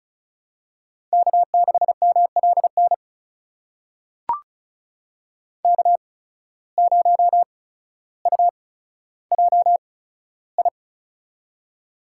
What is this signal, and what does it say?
Morse code, heard as a keyed tone, about 36 wpm, spaced as for 15 wpm, 700 Hz. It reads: K6MLN K0UJI
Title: USA Call Signs - 8x Character Spacing - 35wpm